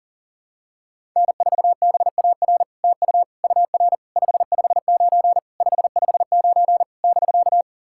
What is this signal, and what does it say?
Morse code, heard as a keyed tone, about 40 wpm, 700 Hz. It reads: N4BAR TU UR 559 559 <BK>
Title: SOTA QSO Practice 40wpm